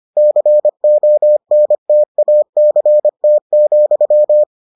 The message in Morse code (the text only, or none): CONTACT,